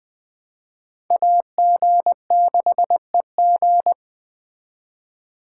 Morse code, heard as a keyed tone, about 22 wpm, 700 Hz. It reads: AG6EG